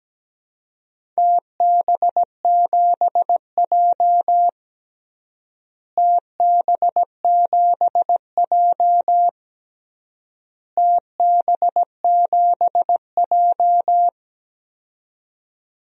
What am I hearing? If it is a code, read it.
Morse code, heard as a keyed tone, about 17 wpm, 700 Hz. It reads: TB7J TB7J TB7J
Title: DXCC-100 Call Signs - 4 Characters - Repeated 3x - 17wpm